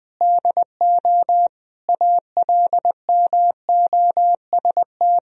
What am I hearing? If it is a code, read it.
Morse code, heard as a keyed tone, about 21 wpm, 700 Hz. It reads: DO ALMOST
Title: Sentences from Top 1000 Words - Review Part 4 - 20wpm